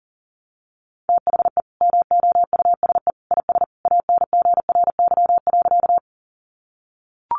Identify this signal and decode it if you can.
Morse code, heard as a keyed tone, about 40 wpm, 700 Hz. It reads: THE MOUSE IS ANGRY.
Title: Easy Phrases 40wpm